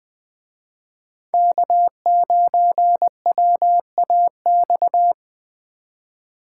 Morse code, heard as a keyed tone, about 21 wpm, 700 Hz. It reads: K9WAX
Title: Speed Racing - USA 1x3 Call Signs 17wpm